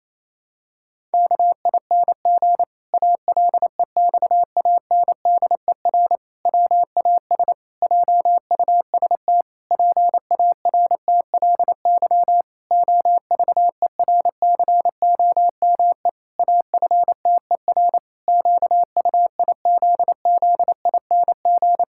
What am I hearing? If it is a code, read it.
Morse code, heard as a keyed tone, about 28 wpm, 700 Hz. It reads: KING ALEXANDER WAS JUST PARTLY OVERCOME AFTER QUIZZING
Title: Pangram Challenge v2 - 28wpm